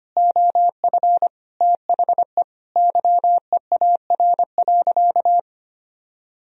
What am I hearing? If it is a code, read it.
Morse code, heard as a keyed tone, about 25 wpm, 700 Hz. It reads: OF THE YEAR.
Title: Sentences from Top 1000 Words - Review Part 1 - 25wpm